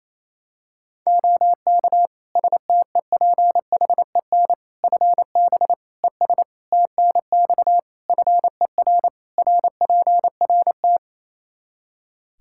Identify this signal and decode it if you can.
Morse code, heard as a keyed tone, about 27 wpm, 700 Hz. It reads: OK STEPHEN FB ES TNX FER RPRT
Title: Practice QSO - Exchange 2 - Info - 28wpm